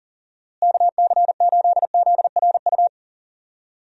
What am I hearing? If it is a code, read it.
Morse code, heard as a keyed tone, about 40 wpm, 700 Hz. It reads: KC8ZRU
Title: USA Call Signs - 6 Characters - Repeated 3x - 40wpm